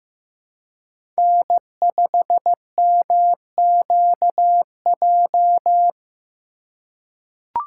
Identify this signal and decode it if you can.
Morse code, heard as a keyed tone, about 15 wpm, 700 Hz. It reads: N5MQJ
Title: USA Call Signs - 5 Characters - 2x Character Spacing - 15wpm